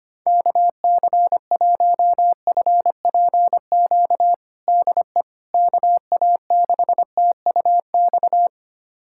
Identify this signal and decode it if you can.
Morse code, heard as a keyed tone, about 25 wpm, 700 Hz. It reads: KC1FPQ DE KA6TUX
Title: USA Callsign DE Callsign 25wpm